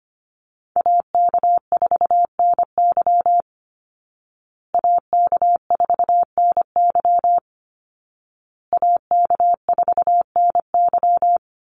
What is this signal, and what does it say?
Morse code, heard as a keyed tone, about 25 wpm, 700 Hz. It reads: AK4NY AK4NY AK4NY
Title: Call Signs Repeated 2x - 25wpm